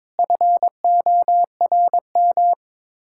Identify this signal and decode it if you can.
Morse code, heard as a keyed tone, about 22 wpm, 700 Hz. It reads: FORM
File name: practice / 22wpm / Sets of 7 Words - 22wpm